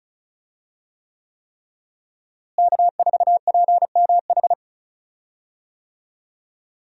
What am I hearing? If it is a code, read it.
Morse code, heard as a keyed tone, about 34 wpm, 700 Hz. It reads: K4PMH
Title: USA Call Signs - 5 Characters - Repeated 3x - 35wpm